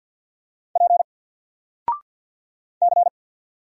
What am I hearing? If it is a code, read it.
Morse code, heard as a keyed tone, about 48 wpm, 700 Hz. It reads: P C
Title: Single Letter-Number 50wpm